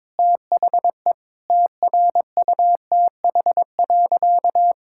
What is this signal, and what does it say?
Morse code, heard as a keyed tone, about 22 wpm, 700 Hz. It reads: THE TRUTH.
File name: Sentences with 7-Word Phrases - 22wpm